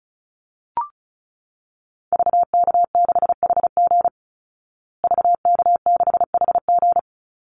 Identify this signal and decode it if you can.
Morse code, heard as a keyed tone, about 34 wpm, 700 Hz. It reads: VK6HG VK6HG
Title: Call Signs Repeated 3x - 35wpm